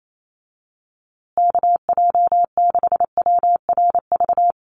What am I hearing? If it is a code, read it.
Morse code, heard as a keyed tone, about 28 wpm, 700 Hz. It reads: KJ6WRV
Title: USA Call Signs Repeated 2x 28wpm